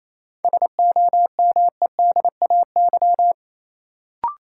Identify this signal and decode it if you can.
Morse code, heard as a keyed tone, about 28 wpm, 700 Hz. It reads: SOMEDAY
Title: Sentences from Top 1600 Words - New Word Drill Alt - 28wpm